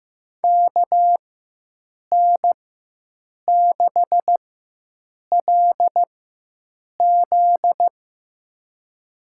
Morse code, heard as a keyed tone, about 15 wpm, spaced as for 7 wpm, 700 Hz. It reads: KN6LZ
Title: USA Call Signs - 4x Character Spacing - 15wpm